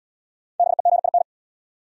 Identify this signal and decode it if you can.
Morse code, heard as a keyed tone, about 71 wpm, 700 Hz. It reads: BEGIN